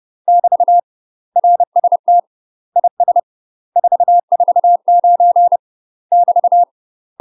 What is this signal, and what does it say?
Morse code, heard as a keyed tone, about 30 wpm, 700 Hz. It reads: <BT> RST IS 449 <BT>
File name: UK MASTER QSOs TRAINING / 30WPM / UK MASTER QSOs 0013